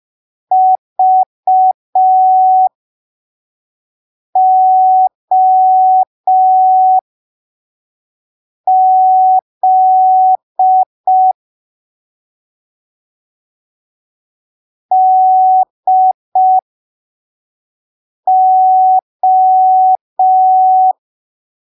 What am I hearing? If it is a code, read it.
Morse code, heard as a keyed tone, about 5 wpm, spaced as for 3 wpm, 750 Hz. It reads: VOZ DO
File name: arquivos / texto / texto 13 05ppm